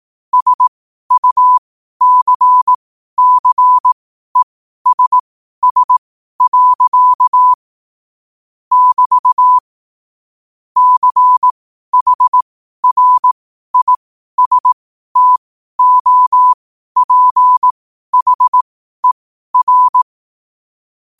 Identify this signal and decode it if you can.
Morse code, heard as a keyed tone, about 18 wpm, spaced as for 12 wpm, 1000 Hz. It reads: SUCCESS. <BT> CHRISTOPHER
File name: qotdfiles / 12wpm